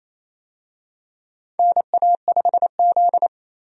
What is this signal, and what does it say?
Morse code, heard as a keyed tone, about 28 wpm, 700 Hz. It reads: NA5Z
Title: USA Call Signs - 4 Characters - 28wpm